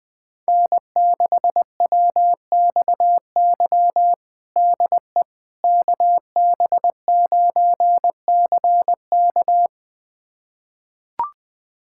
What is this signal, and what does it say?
Morse code, heard as a keyed tone, about 21 wpm, 700 Hz. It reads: N6WXY DE KB9CK
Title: USA Callsign DE Callsign - 5 Characters - 20wpm